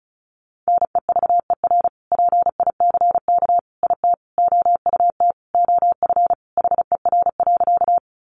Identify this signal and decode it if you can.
Morse code, heard as a keyed tone, about 34 wpm, 700 Hz. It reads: NEVER PICK IT OUT OF HER.
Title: Sentences from Top 1000 Words 35wpm